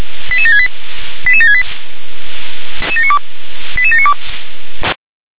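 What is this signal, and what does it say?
Five-tone selective call (ZVEI), heard as quick runs of five tones, tones of about 70 ms. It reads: 88768 88768 88792
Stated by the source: ZVEI five-tone selective call, address 88658, received on an SDR.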